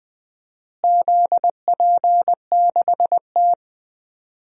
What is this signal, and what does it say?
Morse code, heard as a keyed tone, about 20 wpm, 700 Hz. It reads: ZP6T
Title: DXCC-100 Call Signs - 4 Characters - Repeated 2x - 20wpm